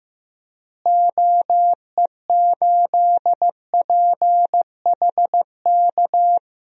Morse code, heard as a keyed tone, about 15 wpm, 700 Hz. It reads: OE8PHK
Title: DXCC-100 Call Signs - 6 Characters - Repeated 3x - 15wpm